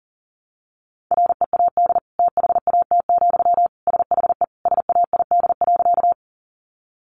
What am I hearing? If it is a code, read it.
Morse code, heard as a keyed tone, about 40 wpm, 700 Hz. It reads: READ THAT, SHE SAID.
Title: Sentences from Top 200 Words 40wpm